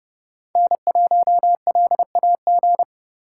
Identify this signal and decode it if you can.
Morse code, heard as a keyed tone, about 28 wpm, 700 Hz. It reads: N1LAG